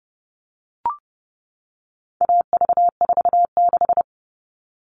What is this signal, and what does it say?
Morse code, heard as a keyed tone, about 30 wpm, 700 Hz. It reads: AV46
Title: All Cut Numbers - 4-Digit 30wpm